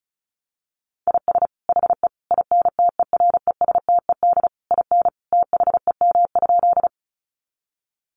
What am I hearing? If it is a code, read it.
Morse code, heard as a keyed tone, about 34 wpm, 700 Hz. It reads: IS HE INTERESTED IN THEM?
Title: Sentences from Top 900 Words 35wpm